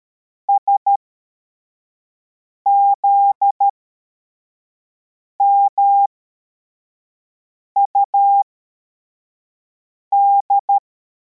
Morse code, heard as a keyed tone, about 13 wpm, spaced as for 4 wpm, 800 Hz. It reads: SZMUD
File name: formation / cw / Text52